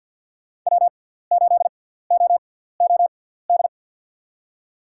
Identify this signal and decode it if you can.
Morse code, heard as a keyed tone, about 47 wpm, spaced as for 17 wpm, 700 Hz. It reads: W8OOD
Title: USA Call Signs - 5 Characters - 6x Character Spacing - 50wpm